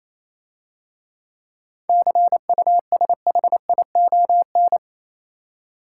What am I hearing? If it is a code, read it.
Morse code, heard as a keyed tone, about 27 wpm, 700 Hz. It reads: CUSHION
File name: Easy Words - Speed-Racing with Hints - 22wpm